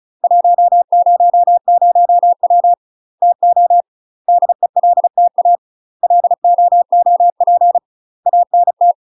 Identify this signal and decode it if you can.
Morse code, heard as a keyed tone, about 36 wpm, 700 Hz. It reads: 100W TO DELTA LOOP ANT
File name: UK MASTER QSOs TRAINING / 35WPM / UK MASTER QSOs 0017